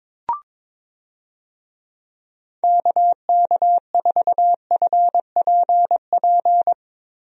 Morse code, heard as a keyed tone, about 22 wpm, 700 Hz. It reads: KK4FPP
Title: USA Call Signs - 6 Characters - Repeated 3x - 22wpm